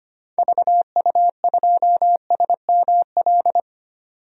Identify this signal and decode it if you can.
Morse code, heard as a keyed tone, about 25 wpm, 700 Hz. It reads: VU2SML
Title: DXCC-100 Call Signs - 6 Characters - 25wpm